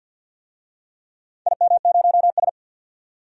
Morse code, heard as a keyed tone, about 47 wpm, 700 Hz. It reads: IM0S
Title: DXCC-100 Call Signs - 4 Characters - 50wpm